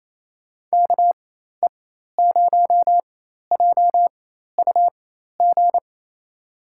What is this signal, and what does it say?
Morse code, heard as a keyed tone, about 28 wpm, spaced as for 13 wpm, 700 Hz. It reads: KE0JUG
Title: USA Call Signs - 4x Character Spacing - 28wpm